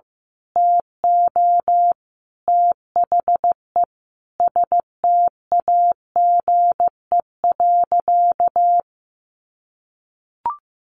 Morse code, heard as a keyed tone, about 15 wpm, 700 Hz. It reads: TO THE STAGE.
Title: Sentences from Top 1200 Words 15wpm